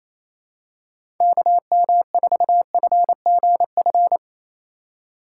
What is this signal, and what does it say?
Morse code, heard as a keyed tone, about 28 wpm, 700 Hz. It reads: KM4FGF
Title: USA Call Signs - 6 Characters - 2x Character Spacing - 28wpm